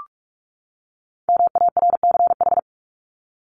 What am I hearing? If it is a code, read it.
Morse code, heard as a keyed tone, about 44 wpm, 700 Hz. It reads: MARCH
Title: Top 1501-1750 Words 45wpm